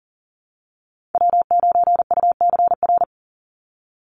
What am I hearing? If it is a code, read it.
Morse code, heard as a keyed tone, about 40 wpm, 700 Hz. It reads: W9UCR
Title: Call Signs - 40wpm